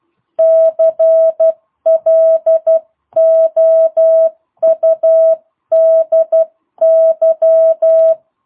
Morse code, heard as a keyed tone, about 12 wpm, 660 Hz. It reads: CLOUDY
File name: jh8cbh / 2022cw